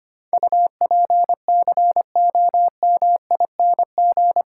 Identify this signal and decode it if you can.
Morse code, heard as a keyed tone, about 25 wpm, 700 Hz. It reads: UPCOMING
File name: Eight Letter Words - 25wpm